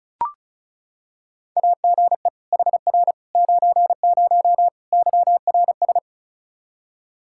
Morse code, heard as a keyed tone, about 34 wpm, 700 Hz. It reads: AGE HR 90 YRS
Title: Practice QSO - Exchange 3 - More Info - 35wpm